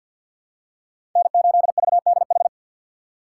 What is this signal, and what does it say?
Morse code, heard as a keyed tone, about 53 wpm, 700 Hz. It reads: N8VDH
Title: USA Call Signs - 5 Characters - 6x Character Spacing - 50wpm